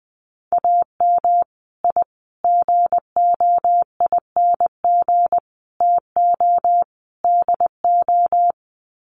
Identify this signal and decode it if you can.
Morse code, heard as a keyed tone, about 20 wpm, 700 Hz. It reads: AM I GOING TO DO
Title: Sets of 5 Words Encore 20wpm